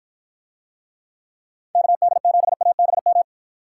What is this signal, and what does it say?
Morse code, heard as a keyed tone, about 53 wpm, 700 Hz. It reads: KD7ABM